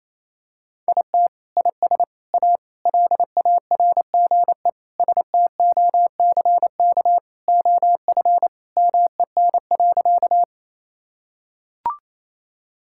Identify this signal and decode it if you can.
Morse code, heard as a keyed tone, about 28 wpm, 700 Hz. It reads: IT IS A LARGE STOCK OF MEN.
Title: Sentences from Top 1000 Words - Review Part 5 - 28wpm